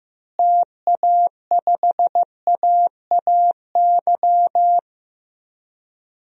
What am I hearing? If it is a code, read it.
Morse code, heard as a keyed tone, about 15 wpm, 700 Hz. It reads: TA5AAY
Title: DXCC-100 Call Signs - 6 Characters - Repeated 2x - 15wpm